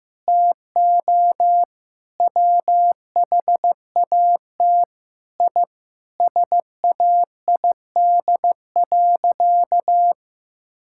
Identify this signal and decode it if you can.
Morse code, heard as a keyed tone, about 15 wpm, 700 Hz. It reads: TO WHAT I SAID.
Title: Sentences from Top 1000 Words - Review Part 6 - 15wpm